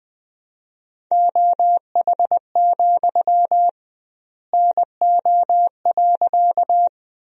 Morse code, heard as a keyed tone, about 20 wpm, 700 Hz. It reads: OH, NO.
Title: 2x Word-Spacing - Sentences from Top 500 Words - 20wpm